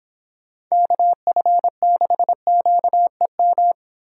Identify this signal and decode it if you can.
Morse code, heard as a keyed tone, about 26 wpm, 700 Hz. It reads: KF6QEM